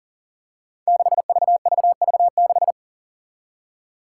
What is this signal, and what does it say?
Morse code, heard as a keyed tone, about 40 wpm, 700 Hz. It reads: 6VVV6
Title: Five Challenging Letter-Number Combinations 40wpm